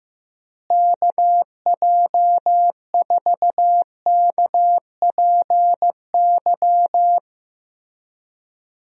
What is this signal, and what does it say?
Morse code, heard as a keyed tone, about 15 wpm, 700 Hz. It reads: KJ4KPY
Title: USA Call Signs - 6 Characters - Repeated 2x - 15wpm